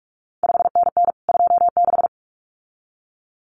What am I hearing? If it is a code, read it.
Morse code, heard as a keyed tone, about 46 wpm, 700 Hz. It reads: 5NN 26
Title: CQ WW DX Contest Exchange Encore - 45wpm